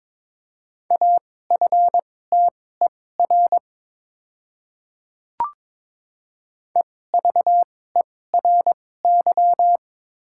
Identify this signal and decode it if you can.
Morse code, heard as a keyed tone, about 22 wpm, spaced as for 16 wpm, 700 Hz. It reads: AFTER EVERY
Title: 2x Character Spacing - Top 300 Words - 22wpm